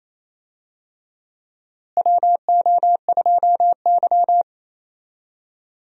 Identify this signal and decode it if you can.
Morse code, heard as a keyed tone, about 28 wpm, 700 Hz. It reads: WO2Y
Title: USA Call Signs - 4 Characters - Repeated 3x - 28wpm